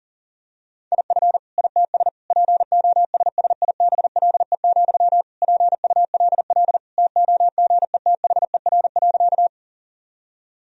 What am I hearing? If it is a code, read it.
Morse code, heard as a keyed tone, about 40 wpm, 700 Hz. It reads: IF ITS POSSIBLE, PULL TOGETHER.
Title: Sentences from Top 1000 Words - Review Part 2 - 40wpm